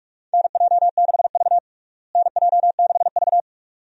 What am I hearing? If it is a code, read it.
Morse code, heard as a keyed tone, about 46 wpm, 700 Hz. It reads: NJ6V NJ6V
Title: USA Call Signs - 4 Characters - Repeated 2x - 45wpm